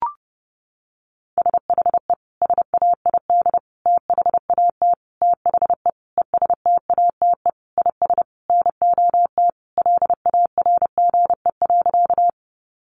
Morse code, heard as a keyed tone, about 30 wpm, 700 Hz. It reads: SHE SAID THAT THE ESTATE IS NOT LARGE.
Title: Sentences from Top 1200 Words 30wpm